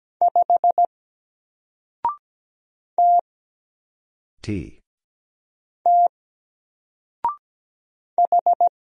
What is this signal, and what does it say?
Morse code, heard as a keyed tone, about 17 wpm, 700 Hz. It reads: 5 T T H
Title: Single Character (Letter:Number:Punctuation:Prosign) 17wpm